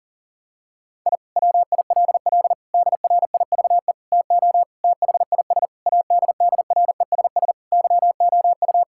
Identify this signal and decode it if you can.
Morse code, heard as a keyed tone, about 40 wpm, 700 Hz. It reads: I WILL DRIVE TO THIS ADDRESS YOU
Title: Sentences from Top 1000 Words - Review Part 6 - 40wpm